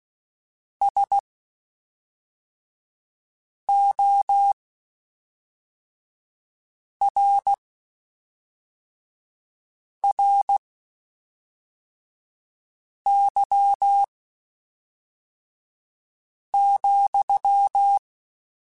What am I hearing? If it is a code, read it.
Morse code, heard as a keyed tone, about 16 wpm, spaced as for 3 wpm, 800 Hz. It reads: SORRY,
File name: (spaced) / QSOs Spaced-mp3s / kc7sac @5 sp